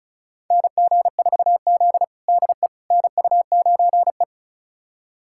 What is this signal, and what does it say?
Morse code, heard as a keyed tone, about 36 wpm, 700 Hz. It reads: NG4Z DE NU9E
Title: USA Callsign DE Callsign - 4 Characters - 35wpm